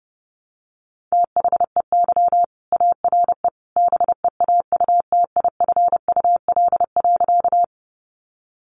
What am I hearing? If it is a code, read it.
Morse code, heard as a keyed tone, about 30 wpm, 700 Hz. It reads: THEY ARE BEAUTIFUL.
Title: Sentences from Top 900 Words 30wpm